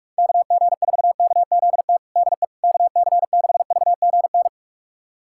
Morse code, heard as a keyed tone, about 46 wpm, 700 Hz. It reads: KG4KZT DE KC6VGN
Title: USA Callsign DE Callsign 45wpm